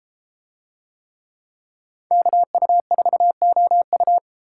Morse code, heard as a keyed tone, about 31 wpm, 700 Hz. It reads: KU4OU